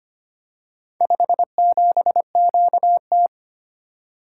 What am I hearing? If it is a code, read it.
Morse code, heard as a keyed tone, about 25 wpm, 700 Hz. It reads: 57QT
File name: practice / 25wpm / Four Letter-Number Combinations - 25wpm